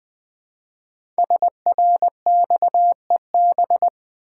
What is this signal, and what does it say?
Morse code, heard as a keyed tone, about 20 wpm, 700 Hz. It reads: SRXEB
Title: Code Groups - 5 Letters - 20wpm